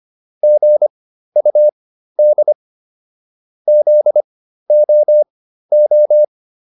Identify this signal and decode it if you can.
Morse code, heard as a keyed tone, about 25 wpm, spaced as for 13 wpm, 600 Hz. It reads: GUD ZOO